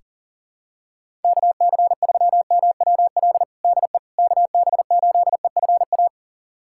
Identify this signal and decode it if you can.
Morse code, heard as a keyed tone, about 40 wpm, 700 Hz. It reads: KC3MWL DE KB8EFA
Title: USA Callsign DE Callsign - 6 Characters - 40wpm